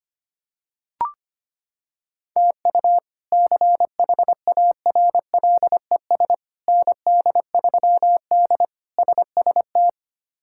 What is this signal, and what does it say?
Morse code, heard as a keyed tone, about 25 wpm, 700 Hz. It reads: TU CHARLES ND3D SST
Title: K1USN SST QSO Practice - 25wpm